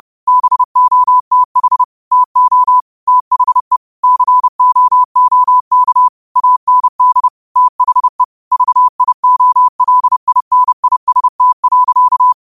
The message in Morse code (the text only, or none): BOTH TO THE COOK AND THE VIOLINIST.